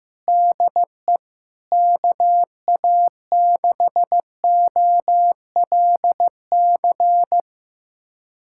Morse code, heard as a keyed tone, about 16 wpm, 700 Hz. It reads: DE KA6OLC
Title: USA Callsign DE Callsign - 6 Characters - 15wpm